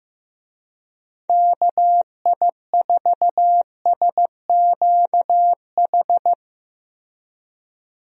Morse code, heard as a keyed tone, about 15 wpm, 700 Hz. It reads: KI4SQH
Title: USA Call Signs - 6 Characters - Repeated 3x - 15wpm